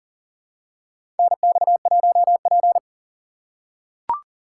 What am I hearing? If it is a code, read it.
Morse code, heard as a keyed tone, about 40 wpm, 700 Hz. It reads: NX1P
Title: USA Call Signs - 4 Characters - 2x Character Spacing - 40wpm